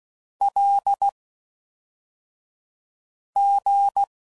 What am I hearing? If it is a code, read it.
Morse code, heard as a keyed tone, about 16 wpm, 800 Hz. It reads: L G